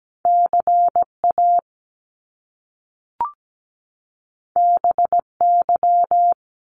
Morse code, heard as a keyed tone, about 17 wpm, 700 Hz. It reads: CA BY
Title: Top Two Letter Combinations in Words 17wpm